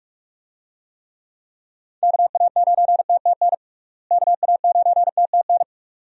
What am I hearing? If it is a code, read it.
Morse code, heard as a keyed tone, about 46 wpm, 700 Hz. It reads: KA9TTN KA9TTN
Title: USA Call Signs - 6 Characters - Repeated 2x - 45wpm